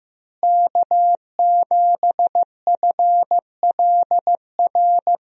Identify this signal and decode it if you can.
Morse code, heard as a keyed tone, about 15 wpm, 700 Hz. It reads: K7FLR
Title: USA Call Signs - 5 Characters - Repeated 3x - 15wpm